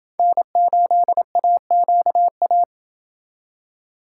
Morse code, heard as a keyed tone, about 28 wpm, 700 Hz. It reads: N8AQA